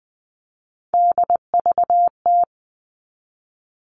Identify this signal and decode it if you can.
Morse code, heard as a keyed tone, about 20 wpm, 700 Hz. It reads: DVT
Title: All Cut Numbers - 3-Digit 20wpm